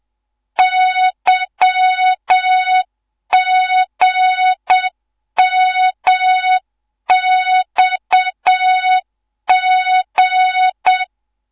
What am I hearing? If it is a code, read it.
Morse code, heard as a keyed tone, about 7 wpm, 3700 Hz. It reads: YGMXG